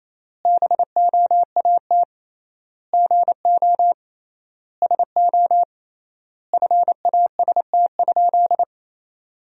Morse code, heard as a keyed tone, about 28 wpm, 700 Hz. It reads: BOAT GO SO FAST?
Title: 3x Word-Spacing - Sentences from Top 500 Words - 28wpm